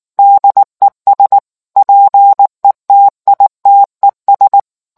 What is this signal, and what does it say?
Morse code, heard as a keyed tone, about 19 wpm, 800 Hz. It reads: DES PETITES